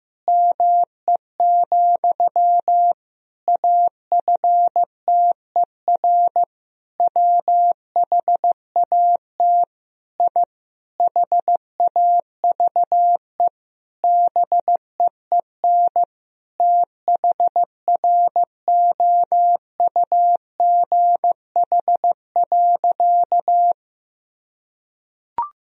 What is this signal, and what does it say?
Morse code, heard as a keyed tone, about 15 wpm, 700 Hz. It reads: ME, AFTER WHAT I HAVE BEEN THROUGH.
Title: Sentences from Top 1000 Words - Review Part 1 - 15wpm